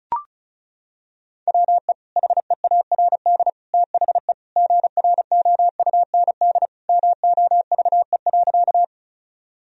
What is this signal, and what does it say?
Morse code, heard as a keyed tone, about 36 wpm, 700 Hz. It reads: WE HEARD THE GROUND MOVE.
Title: Sentences from Top 1000 Words - Review Part 2 - 35wpm